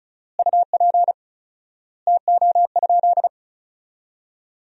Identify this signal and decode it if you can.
Morse code, heard as a keyed tone, about 36 wpm, 700 Hz. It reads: UP TO?
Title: 4x Word-Spacing - Sentences from Top 100 Words - 35wpm